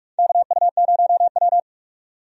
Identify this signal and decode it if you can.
Morse code, heard as a keyed tone, about 41 wpm, 700 Hz. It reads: KU0W
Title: USA Call Signs - 4 Characters - 45wpm